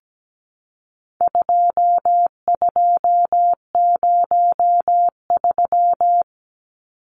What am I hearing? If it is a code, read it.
Morse code, heard as a keyed tone, about 17 wpm, 700 Hz. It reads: 2203
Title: Numbers - 4-Digit 17wpm